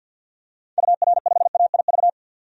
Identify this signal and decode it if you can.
Morse code, heard as a keyed tone, about 50 wpm, 700 Hz. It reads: UR5AIV
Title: DXCC-100 Call Signs - 6 Characters - 50wpm